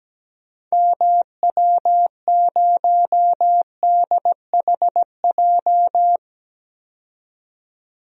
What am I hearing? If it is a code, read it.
Morse code, heard as a keyed tone, about 17 wpm, 700 Hz. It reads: MW0DHJ
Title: DXCC-100 Call Signs - 6 Characters - 17wpm